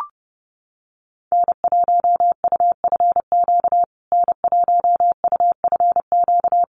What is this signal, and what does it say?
Morse code, heard as a keyed tone, about 30 wpm, 700 Hz. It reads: N1UFQ N1UFQ
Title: USA Call Signs Repeated 3x 30wpm